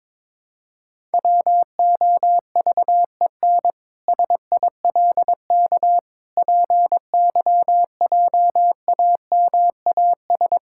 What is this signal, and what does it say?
Morse code, heard as a keyed tone, about 22 wpm, 700 Hz. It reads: WOVEN SILK PYJAMAS
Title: Pangram Challenge v2 - 22wpm